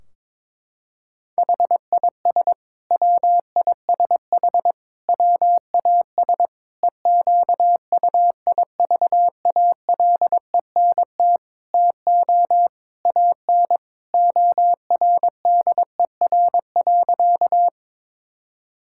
Morse code, heard as a keyed tone, about 22 wpm, 700 Hz. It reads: HIS WISH WAS EQUIVALENT TO AN ORDER.
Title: Sentences from Top 1600 Words 22wpm